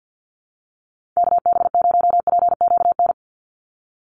Morse code, heard as a keyed tone, about 50 wpm, 700 Hz. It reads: KB0PQN
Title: USA Call Signs 50wpm